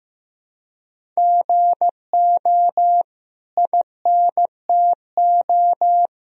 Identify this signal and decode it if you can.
Morse code, heard as a keyed tone, about 15 wpm, 700 Hz. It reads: GO INTO
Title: Sentences from Top 1000 Words - Review Part 1 - 15wpm